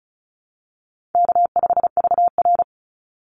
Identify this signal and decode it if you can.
Morse code, heard as a keyed tone, about 34 wpm, 700 Hz. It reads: K5VR
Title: Easy Call Signs Repeated 3x - 35wpm